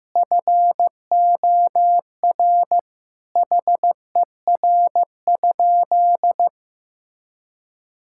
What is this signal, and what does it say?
Morse code, heard as a keyed tone, about 16 wpm, 700 Hz. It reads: FOR HER?
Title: Sentences from Top 1600 Words 15wpm